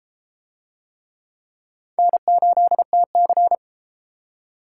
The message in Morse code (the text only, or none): N8TC